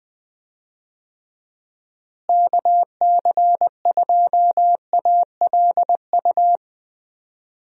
Morse code, heard as a keyed tone, about 20 wpm, 700 Hz. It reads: KC2ALU
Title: USA Call Signs - 6 Characters - Repeated 2x - 20wpm